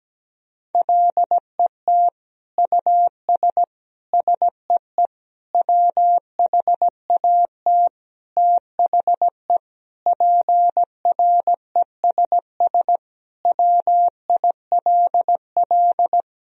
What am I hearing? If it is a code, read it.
Morse code, heard as a keyed tone, about 17 wpm, 700 Hz. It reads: LET US SEE WHAT THE PRESS WILL
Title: Sentences from Top 1000 Words - Review Part 1 - 17wpm